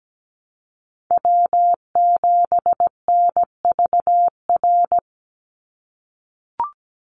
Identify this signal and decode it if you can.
Morse code, heard as a keyed tone, about 17 wpm, 700 Hz. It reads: W7NVR
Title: USA Call Signs 17wpm